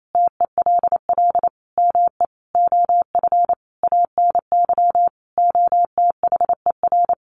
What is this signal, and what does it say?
Morse code, heard as a keyed tone, about 28 wpm, 700 Hz. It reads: TELL ME OF ANY OTHER
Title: Sentences from Top 1400 Words 28wpm